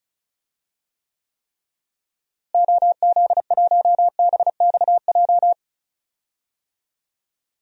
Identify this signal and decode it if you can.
Morse code, heard as a keyed tone, about 36 wpm, 700 Hz. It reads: OZ1BXJ
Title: DXCC-100 Call Signs - 6 Characters - Repeated 3x - 35wpm